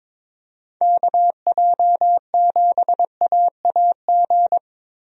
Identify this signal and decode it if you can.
Morse code, heard as a keyed tone, about 22 wpm, 700 Hz. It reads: KJ7AAG
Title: USA Call Signs - 6 Characters - Repeated 2x - 22wpm